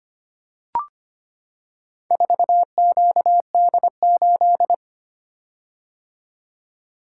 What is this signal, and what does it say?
Morse code, heard as a keyed tone, about 25 wpm, 700 Hz. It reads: 4QD8
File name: practice / 25wpm / Four Letter-Number Combinations - 25wpm